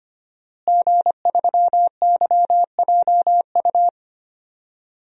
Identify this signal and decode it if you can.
Morse code, heard as a keyed tone, about 25 wpm, 700 Hz. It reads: G3YJU
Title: DXCC-100 Call Signs - 5 Characters - Repeated 3x - 25wpm